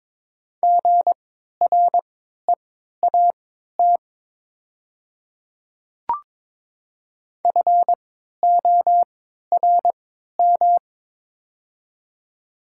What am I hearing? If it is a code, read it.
Morse code, heard as a keyed tone, about 23 wpm, spaced as for 13 wpm, 700 Hz. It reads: GREAT FORM
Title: 3x Character Spacing - Top 300 Words - 22wpm